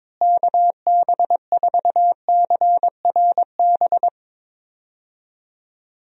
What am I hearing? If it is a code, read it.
Morse code, heard as a keyed tone, about 23 wpm, 700 Hz. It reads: KB4CRB
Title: USA Call Signs - 6 Characters - Repeated 2x - 22wpm